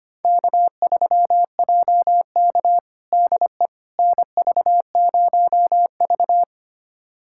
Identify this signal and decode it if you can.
Morse code, heard as a keyed tone, about 25 wpm, 700 Hz. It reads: K3JK DE NV0V
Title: USA Callsign DE Callsign - 4 Characters - 25wpm